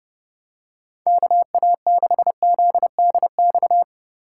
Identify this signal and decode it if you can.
Morse code, heard as a keyed tone, about 32 wpm, 700 Hz. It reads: KA6ZDX
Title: USA Call Signs - 6 Characters - 3x Character Spacing - 30wpm